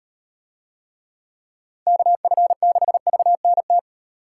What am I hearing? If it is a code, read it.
Morse code, heard as a keyed tone, about 38 wpm, 700 Hz. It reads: KF6VNT